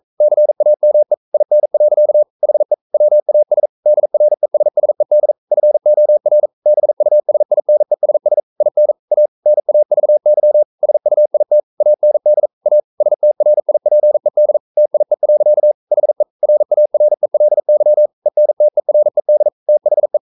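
Morse code, heard as a keyed tone, about 42 wpm, 600 Hz. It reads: CAME IN. HE WAS DRESSED FOR BUSINESS IN A NAVY SUIT AND A STRIPED TIE. HE RARELY ENTERED THE
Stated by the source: Morse code practice at 41 wpm, from CW med Gnister.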